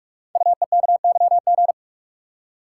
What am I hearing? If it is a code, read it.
Morse code, heard as a keyed tone, about 45 wpm, 700 Hz. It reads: UEKYG
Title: Code Groups - 5 Letters - 45wpm